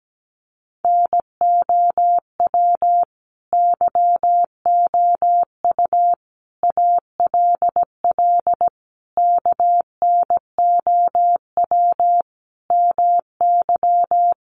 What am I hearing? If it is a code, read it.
Morse code, heard as a keyed tone, about 17 wpm, 700 Hz. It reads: NOW YOU ALL KNOW MY
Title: Sentences from Top 1200 Words 17wpm